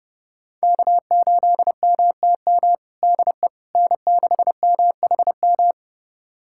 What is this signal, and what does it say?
Morse code, heard as a keyed tone, about 30 wpm, 700 Hz. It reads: K8MTM DE N6MHM
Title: USA Callsign DE Callsign - 5 Characters - 30wpm